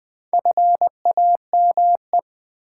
Morse code, heard as a keyed tone, about 20 wpm, 700 Hz. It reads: FAME
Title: Build A Word - 2 Letters At A Time - 4-Letter Words - 20wpm